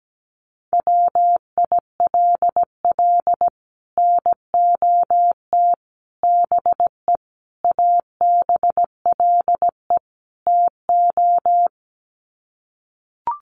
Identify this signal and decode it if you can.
Morse code, heard as a keyed tone, about 17 wpm, 700 Hz. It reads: WILL NOT BE ABLE TO
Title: Sets of 5 Words 17wpm